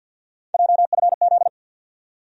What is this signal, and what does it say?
Morse code, heard as a keyed tone, about 53 wpm, 700 Hz. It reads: JFZ
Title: USA Call Sign Postfixes - 3 Letters 50wpm